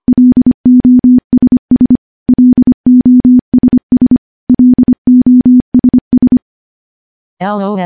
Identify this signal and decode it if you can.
Morse code, heard as a keyed tone, about 25 wpm, 250 Hz. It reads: LOSS LOSS LOSS
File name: morse / morse-training-25WPM